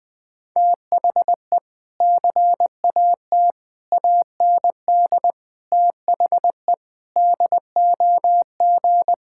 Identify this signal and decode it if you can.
Morse code, heard as a keyed tone, about 20 wpm, 700 Hz. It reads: THE CAT AND THE DOG
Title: Sentences with 2-Word Phrases - 20wpm